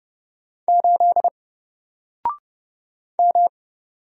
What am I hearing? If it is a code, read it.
Morse code, heard as a keyed tone, about 30 wpm, 700 Hz. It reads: 8 M
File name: Single-Letter-Number-30wpm